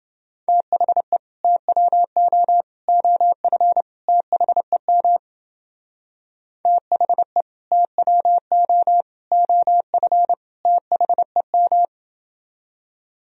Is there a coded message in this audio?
Yes: Morse code from 0.5 to 11.8 s, heard as a keyed tone, about 28 wpm, 700 Hz. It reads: THE TWO OF THEM THE TWO OF THEM